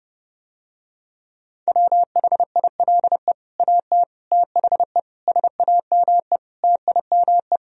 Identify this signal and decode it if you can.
Morse code, heard as a keyed tone, about 30 wpm, 700 Hz. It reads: WHILE AT THE SAME TIME